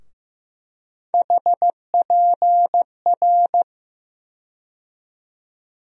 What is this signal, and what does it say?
Morse code, heard as a keyed tone, about 15 wpm, 700 Hz. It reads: HPR
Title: USA Call Sign Postfixes - 3 Letters 15wpm